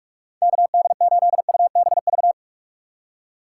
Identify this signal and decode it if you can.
Morse code, heard as a keyed tone, about 43 wpm, 700 Hz. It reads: KD8UBV
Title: USA Call Signs - 6 Characters - 3x Character Spacing - 45wpm